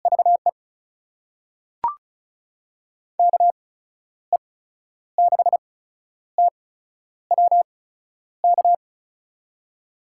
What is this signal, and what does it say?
Morse code, heard as a keyed tone, about 36 wpm, spaced as for 11 wpm, 700 Hz. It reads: VE KE6TWK
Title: USA Call Signs - 6 Characters - 8x Character Spacing - 35wpm